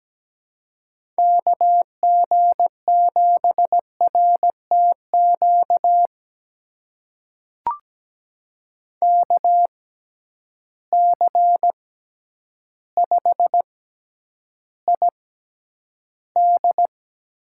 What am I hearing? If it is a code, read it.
Morse code, heard as a keyed tone, about 17 wpm, spaced as for 9 wpm, 700 Hz. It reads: KG7RTQ KC5ID
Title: USA Call Signs - 6 Characters - 6x Character Spacing - 17wpm